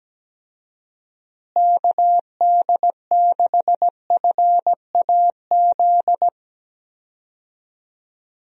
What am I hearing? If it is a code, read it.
Morse code, heard as a keyed tone, about 17 wpm, 700 Hz. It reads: KD6FAZ
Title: USA Call Signs - 6 Characters - Repeated 2x - 17wpm